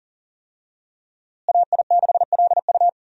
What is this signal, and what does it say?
Morse code, heard as a keyed tone, about 39 wpm, 700 Hz. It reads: AI6LU
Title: USA Call Signs - 5 Characters - Repeated 3x - 40wpm